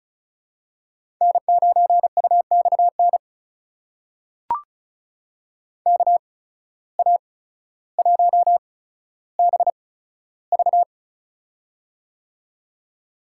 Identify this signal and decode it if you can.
Morse code, heard as a keyed tone, about 36 wpm, spaced as for 15 wpm, 700 Hz. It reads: N9UXN KA1BV
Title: USA Call Signs - 8x Character Spacing - 35wpm